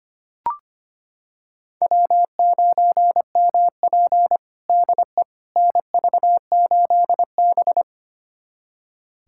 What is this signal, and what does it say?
Morse code, heard as a keyed tone, about 25 wpm, 700 Hz. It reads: W9MP DE NV8B
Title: USA Callsign DE Callsign - 4 Characters - 25wpm